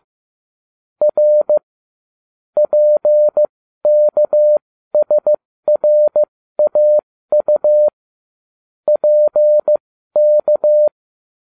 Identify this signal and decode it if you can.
Morse code, heard as a keyed tone, about 15 wpm, spaced as for 13 wpm, 600 Hz. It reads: R PKSRAU PK